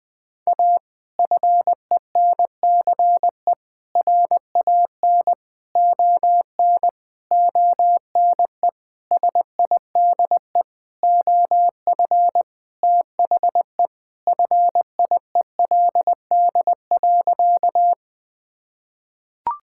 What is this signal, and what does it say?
Morse code, heard as a keyed tone, about 20 wpm, 700 Hz. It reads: A FENCE RAN ON ONE SIDE OF THE FIELD.
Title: Sentences with 5-Word Phrases - 20wpm